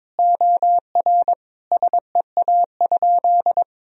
Morse code, heard as a keyed tone, about 22 wpm, 700 Hz. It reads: OR SEA?
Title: Sentences from Top 1000 Words - Review Part 1 - 22wpm